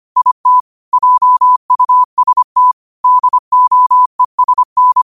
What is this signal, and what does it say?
Morse code, heard as a keyed tone, about 25 wpm, 1000 Hz. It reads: IT JUST DOESN
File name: qotdfiles / 25wpm